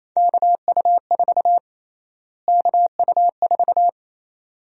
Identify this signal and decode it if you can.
Morse code, heard as a keyed tone, about 28 wpm, 700 Hz. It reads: KU4 KU4
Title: USA Call Sign Prefixes - Repeated 2x - 28wpm